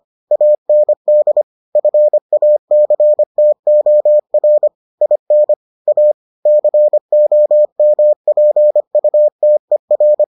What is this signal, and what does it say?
Morse code, heard as a keyed tone, about 25 wpm, 600 Hz. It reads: AND FACTOR IN A COMPUTER